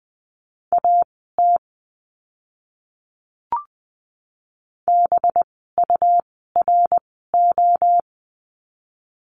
Morse code, heard as a keyed tone, about 20 wpm, spaced as for 15 wpm, 700 Hz. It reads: AT BURO
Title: 185 Lesson 30f - Letter b - Words 20wpm